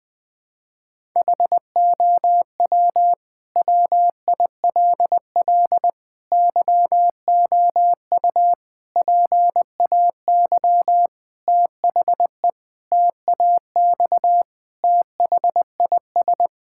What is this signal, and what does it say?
Morse code, heard as a keyed tone, about 19 wpm, 700 Hz. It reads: HOW WILL YOU PAY THE TAX THIS
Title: Sentences from Top 1000 Words - Review Part 5 - 20wpm